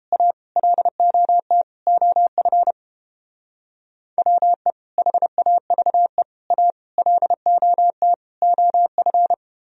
Morse code, heard as a keyed tone, about 29 wpm, 700 Hz. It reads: A LOT OF WE HAVE A LOT OF